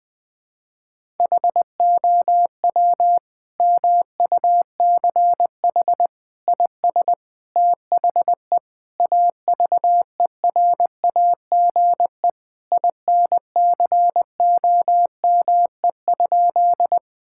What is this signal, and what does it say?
Morse code, heard as a keyed tone, about 20 wpm, 700 Hz. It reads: HOW MUCH IS THE AVERAGE INCOME?
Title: Sentences from Top 1000 Words - Review Part 6 - 20wpm